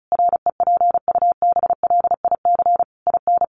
Morse code, heard as a keyed tone, about 34 wpm, 700 Hz. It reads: REPUBLIC IN
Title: Sentences from Top 1500 Words 35wpm